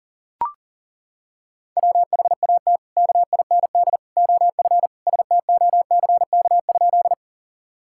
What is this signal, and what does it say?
Morse code, heard as a keyed tone, about 40 wpm, 700 Hz. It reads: WHAT KIND OF STOCK?
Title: Sentences from Top 1000 Words - Review Part 5 - 40wpm